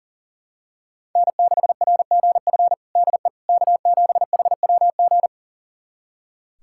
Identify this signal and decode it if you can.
Morse code, heard as a keyed tone, about 37 wpm, 700 Hz. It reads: N6RGF DE K7HWG
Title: USA Callsign DE Callsign - 5 Characters - 40wpm